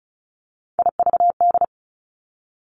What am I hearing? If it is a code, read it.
Morse code, heard as a keyed tone, about 36 wpm, 700 Hz. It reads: IVD
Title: Three Challenging Letter-Number Combinations 35wpm